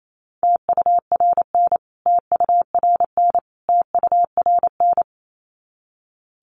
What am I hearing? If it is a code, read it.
Morse code, heard as a keyed tone, about 28 wpm, 700 Hz. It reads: TURN TURN TURN
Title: Top 300 Words Repeated 28wpm